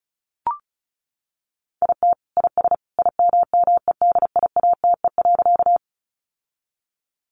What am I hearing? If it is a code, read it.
Morse code, heard as a keyed tone, about 34 wpm, 700 Hz. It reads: IT IS IMMEDIATE.
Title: Sentences from Top 1200 Words 35wpm